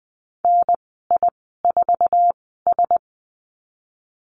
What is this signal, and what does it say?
Morse code, heard as a keyed tone, about 20 wpm, spaced as for 15 wpm, 700 Hz. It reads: NI4S
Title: 048 Lesson 9g - Number 4 - Call Signs 20wpm